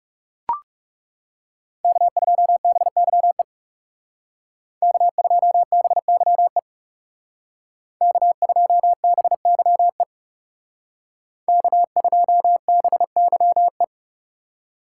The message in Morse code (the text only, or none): K2BYE K2BYE K2BYE K2BYE